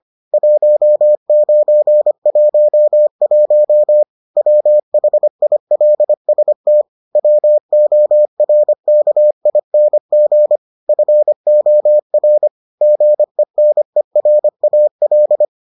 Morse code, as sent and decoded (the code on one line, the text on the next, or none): .---- ----. .---- .---- / .-- .... .. .-.. ... - / .-- --- .-. -.- .. -. --. / ..-. --- .-. / --. -. .-. .- .-..
1911 WHILST WORKING FOR GENERAL